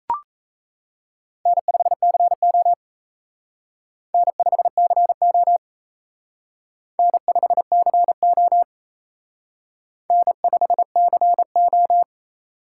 Morse code, heard as a keyed tone, about 35 wpm, 700 Hz. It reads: N5CO N5CO N5CO N5CO
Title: Speed Racing - USA 1x2 Call Signs 28wpm